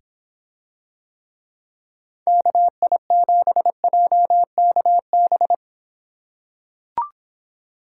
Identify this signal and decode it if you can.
Morse code, heard as a keyed tone, about 26 wpm, 700 Hz. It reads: KI7JKB